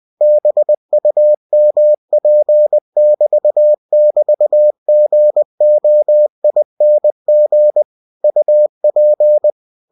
Morse code, heard as a keyed tone, about 20 wpm, 600 Hz. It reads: BUMP<BT><BT>GOING UP
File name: cw-books / winnie-the-pooh / winnie-20wpm-page-0111